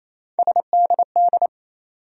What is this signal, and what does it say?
Morse code, heard as a keyed tone, about 28 wpm, 700 Hz. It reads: SDD
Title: USA Call Sign Postfixes - 3 Letters 28wpm